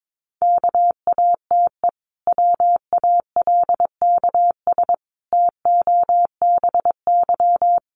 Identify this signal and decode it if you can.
Morse code, heard as a keyed tone, about 22 wpm, 700 Hz. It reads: KATE WALKS TOBY
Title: Easy Phrases 22wpm